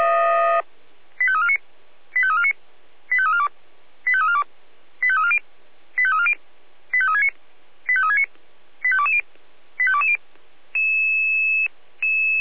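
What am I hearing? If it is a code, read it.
Five-tone selective call (ZVEI), heard as quick runs of five tones, tones of about 70 ms. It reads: 85349 85349 85342 85342 85340 85340 85369 85369 85220 85220
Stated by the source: ZVEI five-tone selective call 85360, received with an SDR.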